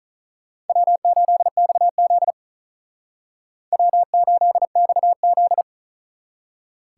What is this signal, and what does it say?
Morse code, heard as a keyed tone, about 45 wpm, 700 Hz. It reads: W8XZ W8XZ